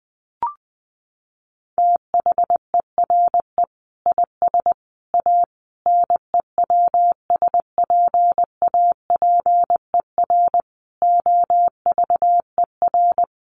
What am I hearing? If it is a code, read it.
Morse code, heard as a keyed tone, about 20 wpm, 700 Hz. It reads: THERE IS A NEWSPAPER OVER
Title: Sentences from Top 1100 Words 20wpm